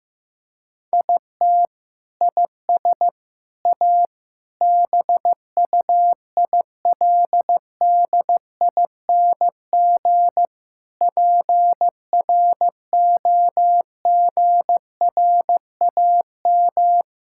Morse code, heard as a keyed tone, about 15 wpm, 700 Hz. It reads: IT IS A BUILDING PROGRAM
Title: Sentences from Top 1000 Words - Review Part 5 - 15wpm